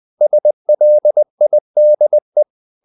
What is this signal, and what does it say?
Morse code, heard as a keyed tone, about 20 wpm, 600 Hz. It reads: SLIDE
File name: the old man and the sea part 2 20wpm